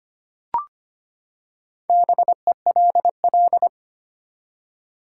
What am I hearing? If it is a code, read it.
Morse code, heard as a keyed tone, about 25 wpm, 700 Hz. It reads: BELL
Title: Four Letter Words - 25wpm